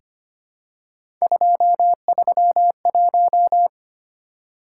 Morse code, heard as a keyed tone, about 26 wpm, 700 Hz. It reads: 231
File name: Three Letter-Number Combinations - 25wpm